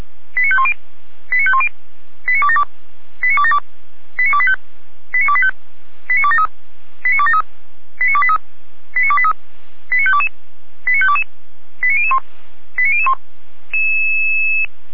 Five-tone selective call (ZVEI), heard as quick runs of five tones, tones of about 70 ms. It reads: 89510 89510 89272 89272 89285 89285 89273 89273 89283 89283 89522 89522 89001 89001